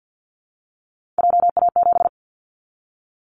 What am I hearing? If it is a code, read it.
Morse code, heard as a keyed tone, about 50 wpm, 700 Hz. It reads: JA7
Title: Callsign Prefixes for Top 50 DXCC - 50wpm